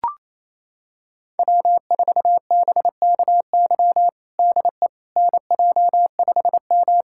Morse code, heard as a keyed tone, about 28 wpm, 700 Hz. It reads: W4BKY DE NJ5M
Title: USA Callsign DE Callsign 28wpm